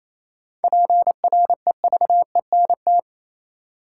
Morse code, heard as a keyed tone, about 28 wpm, 700 Hz. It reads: PREVENT
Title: Sentences from Top 1100 Words - New Word Drill Alt - 28wpm